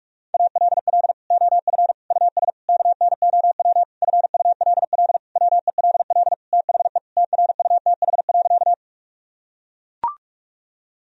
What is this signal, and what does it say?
Morse code, heard as a keyed tone, about 41 wpm, 700 Hz. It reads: ALL OF US KNOW FULL WELL THE TRUTH.
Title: Sentences from Top 1000 Words - Review Part 3 - 45wpm